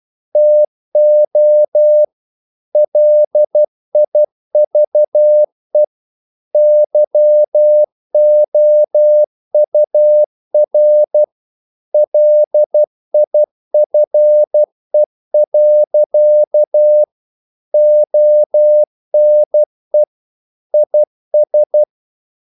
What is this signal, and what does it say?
Morse code, heard as a keyed tone, about 12 wpm, 600 Hz. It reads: TO LIVE YOUR LIFE. ONE IS
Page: English quotations in CW (telegraphy - morse)